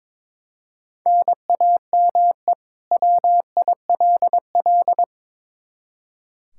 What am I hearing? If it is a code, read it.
Morse code, heard as a keyed tone, about 22 wpm, 700 Hz. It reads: NAME WILL
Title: K1USN SST QSO Practice 22wpm